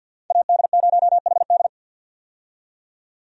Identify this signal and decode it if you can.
Morse code, heard as a keyed tone, about 53 wpm, 700 Hz. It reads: AD0HD
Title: USA Call Signs - 5 Characters - 50wpm